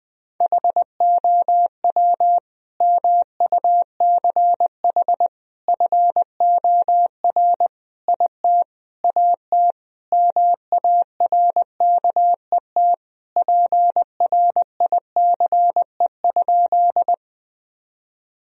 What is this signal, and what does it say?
Morse code, heard as a keyed tone, about 21 wpm, 700 Hz. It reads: HOW MUCH FOR IT AT MARKET PRICE?
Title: Sentences from Top 1000 Words - Review Part 3 - 20wpm